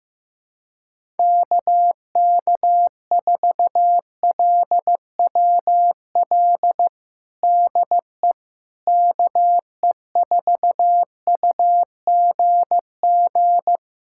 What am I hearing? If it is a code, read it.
Morse code, heard as a keyed tone, about 15 wpm, 700 Hz. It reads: KK4LWL DE KE4UGG
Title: USA Callsign DE Callsign - 6 Characters - 15wpm